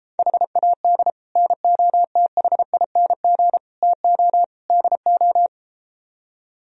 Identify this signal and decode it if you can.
Morse code, heard as a keyed tone, about 33 wpm, 700 Hz. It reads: HAD NOTHING TO DO